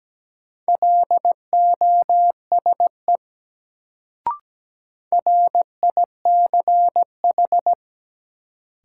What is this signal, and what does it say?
Morse code, heard as a keyed tone, about 17 wpm, 700 Hz. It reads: LOSE RICH
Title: Sentences from Top 1100 Words - New Word Drill - 17wpm